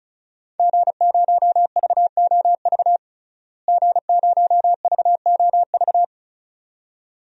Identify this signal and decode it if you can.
Morse code, heard as a keyed tone, about 34 wpm, 700 Hz. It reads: G0VOV G0VOV
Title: DXCC-100 Call Signs - 5 Characters - Repeated 2x - 35wpm